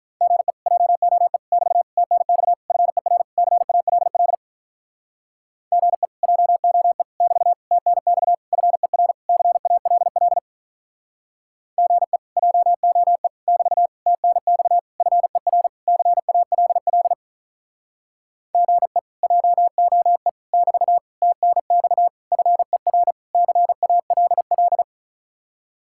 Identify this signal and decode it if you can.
Morse code, heard as a keyed tone, about 45 wpm, 700 Hz. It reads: GE JOE <BT> TNX FER CALL GE JOE <BT> TNX FER CALL GE JOE <BT> TNX FER CALL GE JOE <BT> TNX FER CALL